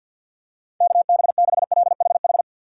Morse code, heard as a keyed tone, about 53 wpm, 700 Hz. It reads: KB6LHH
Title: USA Call Signs - 6 Characters - 4x Character Spacing - 50wpm